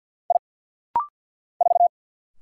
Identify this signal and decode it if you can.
Morse code, heard as a keyed tone, about 50 wpm, 700 Hz. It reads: I 4
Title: Single Letter-Number Alt - Rapid-Fire 50wpm